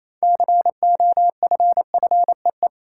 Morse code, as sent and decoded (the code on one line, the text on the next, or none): -.-. --- ..-. ..-. .
COFFEE